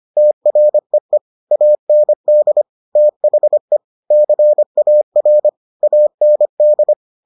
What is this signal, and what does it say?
Morse code, heard as a keyed tone, about 25 wpm, 600 Hz. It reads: TREE AND THE CAR AND